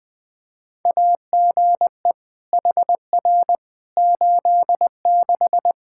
Morse code, heard as a keyed tone, about 20 wpm, 700 Hz. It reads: AGE HR 86
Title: Practice QSO - Exchange 3 - More Info - 20wpm